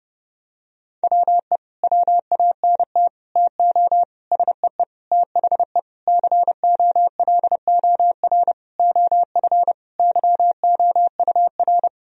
Morse code, heard as a keyed tone, about 31 wpm, 700 Hz. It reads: WE WANT TO SEE THE COLOR OF YOUR
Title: Sentences from Top 1000 Words - Review Part 2 - 30wpm